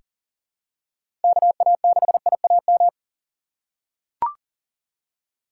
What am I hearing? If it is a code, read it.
Morse code, heard as a keyed tone, about 40 wpm, 700 Hz. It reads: KA6IAM
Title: USA Call Signs - 2x Character Spacing - 40wpm